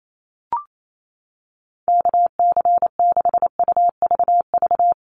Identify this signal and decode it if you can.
Morse code, heard as a keyed tone, about 28 wpm, 700 Hz. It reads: KC6UVV
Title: USA Call Signs 28wpm